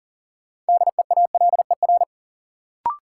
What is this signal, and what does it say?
Morse code, heard as a keyed tone, about 40 wpm, 700 Hz. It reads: DEALER
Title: Sentences from Top 1100 Words - New Word Drill - 40wpm